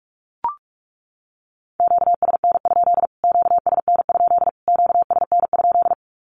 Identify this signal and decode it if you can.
Morse code, heard as a keyed tone, about 44 wpm, 700 Hz. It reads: QSN? QSN? QSN?
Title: Q-Codes Repeated 3x 45wpm